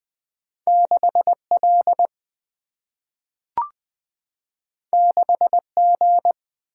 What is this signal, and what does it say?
Morse code, heard as a keyed tone, about 20 wpm, 700 Hz. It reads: 6L 6G
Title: Single-Letter Single-Number 20wpm